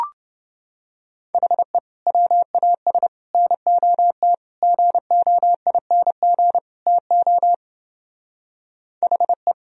Morse code, heard as a keyed tone, about 29 wpm, 700 Hz. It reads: HE WAS NOT GOING TO HE